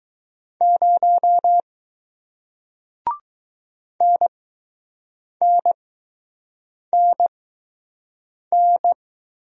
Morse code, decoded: 0 NNNN

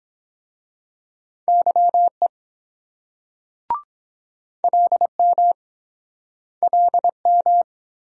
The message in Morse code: YE LM LM

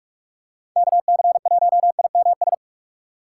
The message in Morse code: KC1IMS

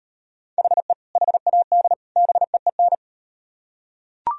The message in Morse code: HE HAD BEEN

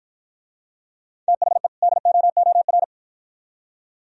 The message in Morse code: THE DOOR

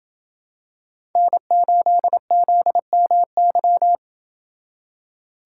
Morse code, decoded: N8ZMY

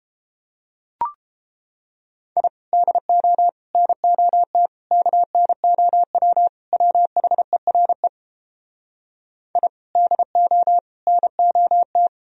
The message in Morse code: I DO NOT KNOW WHERE I DO NOT

